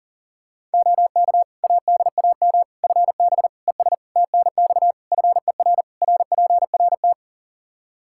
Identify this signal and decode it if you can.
Morse code, heard as a keyed tone, about 40 wpm, 700 Hz. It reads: OK ADAM FB ES TNX FER RPRT
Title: Practice QSO - Exchange 2 - Info - 40wpm